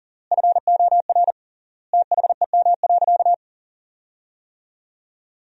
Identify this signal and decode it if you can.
Morse code, heard as a keyed tone, about 40 wpm, 700 Hz. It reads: FOR THEM.
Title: 3x Word-Spacing - Sentences from Top 300 Words - 40wpm